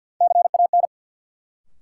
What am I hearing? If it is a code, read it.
Morse code, heard as a keyed tone, about 47 wpm, 700 Hz. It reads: CAN
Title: Sets of 2 Words - New Word Drill - 50wpm